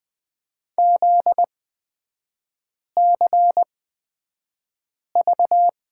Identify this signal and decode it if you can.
Morse code, heard as a keyed tone, about 20 wpm, spaced as for 5 wpm, 700 Hz. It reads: ZCV
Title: 243 Lesson 39e - Letter z - Three Characters 20wpm